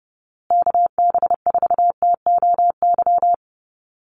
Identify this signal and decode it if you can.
Morse code, heard as a keyed tone, about 30 wpm, 700 Hz. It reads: KB4TOY
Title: USA Call Signs Repeated 2x 30wpm